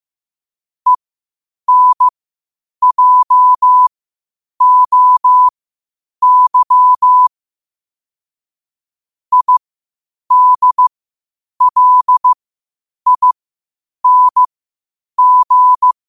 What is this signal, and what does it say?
Morse code, heard as a keyed tone, about 15 wpm, spaced as for 8 wpm, 1000 Hz. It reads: ENJOY IDLING
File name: qotdfiles / 8wpm